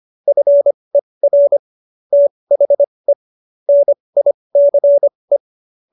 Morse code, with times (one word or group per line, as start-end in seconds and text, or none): 0.3-1.6 FER
2.1-3.1 THE
3.7-5.4 NICE